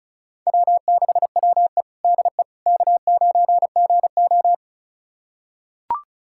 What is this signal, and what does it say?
Morse code, heard as a keyed tone, about 36 wpm, 700 Hz. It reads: W6WE DE K9GO
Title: USA Callsign DE Callsign - 4 Characters - 35wpm